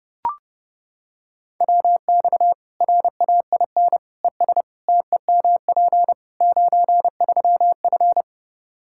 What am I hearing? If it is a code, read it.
Morse code, heard as a keyed tone, about 32 wpm, 700 Hz. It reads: WX RAIN ES TEMP 93F
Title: Practice QSO - Exchange 2 - Info - 30wpm